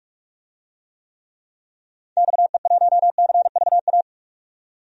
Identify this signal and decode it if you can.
Morse code, heard as a keyed tone, about 46 wpm, 700 Hz. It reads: XE1CVA
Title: DXCC-100 Call Signs - 6 Characters - 45wpm